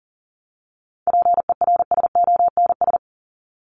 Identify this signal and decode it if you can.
Morse code, heard as a keyed tone, about 40 wpm, 700 Hz. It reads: PERSONS
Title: ES:S Suffix - Top 100 Words with Base Words 40wpm